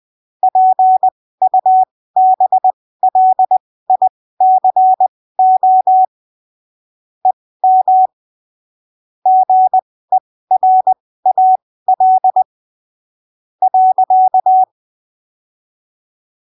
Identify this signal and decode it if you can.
Morse code, heard as a keyed tone, about 20 wpm, spaced as for 14 wpm, 750 Hz. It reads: PUBLICO EM GERAL.
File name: arquivos / texto / texto 01 20ppm